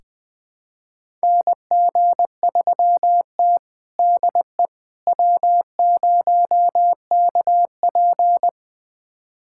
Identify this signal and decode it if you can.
Morse code, heard as a keyed tone, about 20 wpm, 700 Hz. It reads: NG3T DE W0KP
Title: USA Callsign DE Callsign - 4 Characters - 20wpm